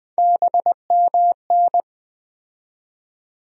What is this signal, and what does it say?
Morse code, heard as a keyed tone, about 20 wpm, 700 Hz. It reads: BMN
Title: USA Call Sign Postfixes - 3 Letters 20wpm